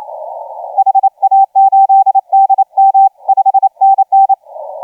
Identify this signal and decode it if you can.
Morse code, heard as a keyed tone, about 28 wpm, 780 Hz, with radio noise. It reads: HA8DM 5NN
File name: JW LY2KW 1502UTC 15m